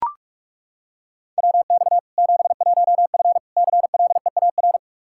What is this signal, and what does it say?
Morse code, heard as a keyed tone, about 42 wpm, 700 Hz. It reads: WX 71F CLEAR
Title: K1USN SST QSO Practice 45wpm